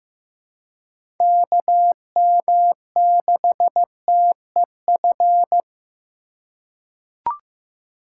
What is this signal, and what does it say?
Morse code, heard as a keyed tone, about 15 wpm, 700 Hz. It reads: KM6TEF
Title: USA Call Signs - 3x Character Spacing - 15wpm